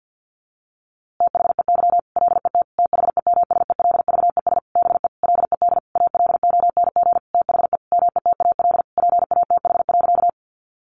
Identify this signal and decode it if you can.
Morse code, heard as a keyed tone, about 50 wpm, 700 Hz. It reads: THEY LET THEMSELVES BE LED ALONG THE METAL PATH.
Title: A Sound Of Thunder 50wpm